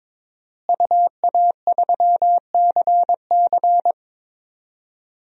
Morse code, heard as a keyed tone, about 22 wpm, 700 Hz. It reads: UA3CC
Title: DXCC-100 Call Signs - 5 Characters - Repeated 2x - 22wpm